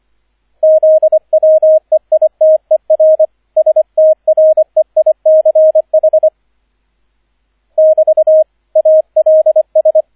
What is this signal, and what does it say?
Morse code, heard as a keyed tone, about 25 wpm, 630 Hz, with radio noise. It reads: ZWEITER STREICH <BT> ALS